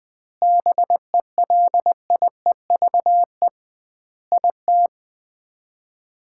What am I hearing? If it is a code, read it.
Morse code, heard as a keyed tone, about 20 wpm, 700 Hz. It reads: BELIEVE IT
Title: 2x Word-Spacing - Sets of 5 Words - 20wpm